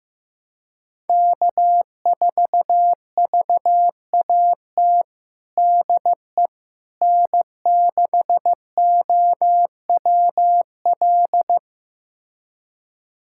Morse code, heard as a keyed tone, about 16 wpm, 700 Hz. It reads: K4VAT DE N6OWL
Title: USA Callsign DE Callsign - 5 Characters - 15wpm